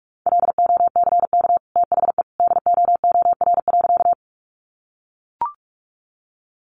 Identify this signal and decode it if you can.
Morse code, heard as a keyed tone, about 44 wpm, 700 Hz. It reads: LOCK THE DOOR.
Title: Sentences from Top 1300 Words 45wpm